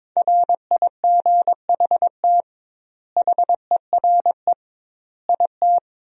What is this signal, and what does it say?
Morse code, heard as a keyed tone, about 22 wpm, 700 Hz. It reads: RIGHT HERE IT
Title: 2x Word-Spacing - Sentences from Top 300 Words - 22wpm